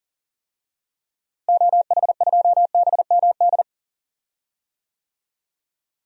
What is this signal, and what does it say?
Morse code, heard as a keyed tone, about 40 wpm, 700 Hz. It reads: OH2BMD
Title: DXCC-100 Call Signs - 6 Characters - Repeated 3x - 40wpm